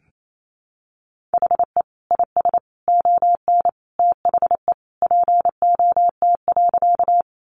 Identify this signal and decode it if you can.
Morse code, heard as a keyed tone, about 28 wpm, 700 Hz. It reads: HE IS ON THE POT.
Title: Sentences from Top 1000 Words 28wpm